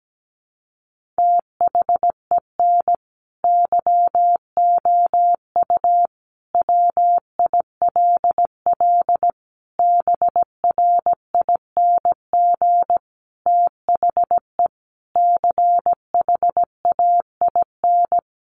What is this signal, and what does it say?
Morse code, heard as a keyed tone, about 17 wpm, 700 Hz. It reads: THEN YOU WILL BRING THE CHAIN
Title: Sentences from Top 1500 Words 17wpm